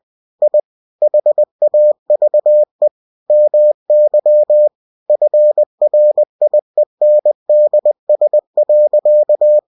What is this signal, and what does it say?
Morse code, heard as a keyed tone, about 20 wpm, 600 Hz. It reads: I HAVE MY FRIENDS.
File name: cw-books / winnie-the-pooh / winnie-20wpm-page-0108